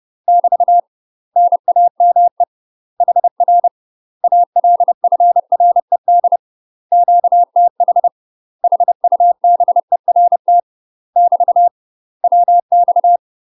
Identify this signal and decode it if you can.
Morse code, heard as a keyed tone, about 30 wpm, 700 Hz. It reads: <BT> NAME HR ALFRED QTH HUBERT <BT> WX
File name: UK MASTER QSOs TRAINING / 30WPM / UK MASTER QSOs 0036